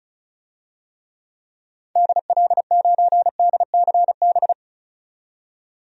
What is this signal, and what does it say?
Morse code, heard as a keyed tone, about 36 wpm, 700 Hz. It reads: DL9DCB
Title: DXCC-100 Call Signs - 6 Characters - 35wpm